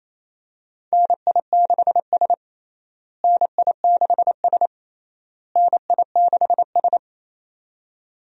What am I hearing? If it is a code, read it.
Morse code, heard as a keyed tone, about 28 wpm, 700 Hz. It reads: NI6S NI6S NI6S
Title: USA Call Signs - 4 Characters - Repeated 3x - 28wpm